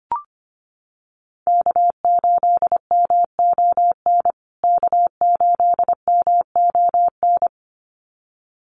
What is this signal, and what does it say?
Morse code, heard as a keyed tone, about 25 wpm, 700 Hz. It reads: K8MON K8MON
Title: USA Call Signs Repeated 2x 25wpm